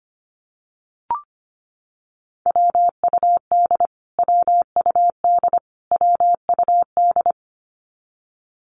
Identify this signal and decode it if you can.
Morse code, heard as a keyed tone, about 25 wpm, 700 Hz. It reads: WUD WUD WUD
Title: Abbreviations Repeated 3x - 25wpm